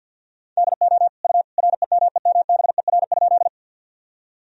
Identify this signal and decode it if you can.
Morse code, heard as a keyed tone, about 50 wpm, 700 Hz. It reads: DO U REMEMBER?
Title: Sentences from Top 400 Words and Abbreviations 50wpm